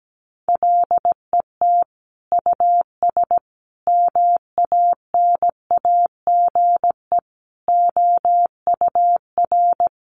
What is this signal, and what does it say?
Morse code, heard as a keyed tone, about 17 wpm, 700 Hz. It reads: LET US MANAGE OUR
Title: Sentences from Top 1400 Words 17wpm